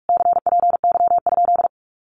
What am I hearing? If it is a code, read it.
Morse code, heard as a keyed tone, about 44 wpm, 700 Hz. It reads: CPY?
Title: Common QSO Phrases Encore 45wpm